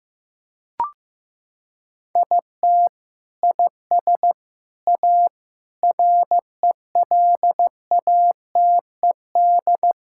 Morse code, heard as a keyed tone, about 15 wpm, 700 Hz. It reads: IT IS A RELATED
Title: Sentences from Top 1000 Words - Review Part 5 - 15wpm